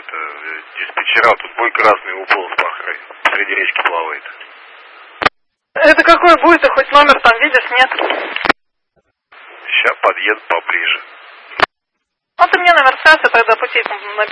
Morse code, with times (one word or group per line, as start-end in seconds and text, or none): none